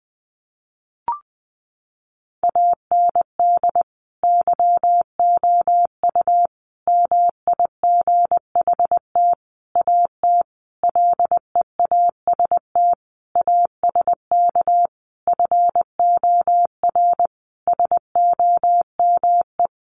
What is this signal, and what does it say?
Morse code, heard as a keyed tone, about 20 wpm, 700 Hz. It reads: AND YOU MIGHT AT LEAST ASK FOR SOME
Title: Sentences from Top 1500 Words 20wpm